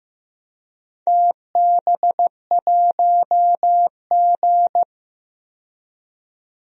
Morse code, heard as a keyed tone, about 15 wpm, 700 Hz. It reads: TB1G
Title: DXCC-100 Call Signs - 4 Characters - 15wpm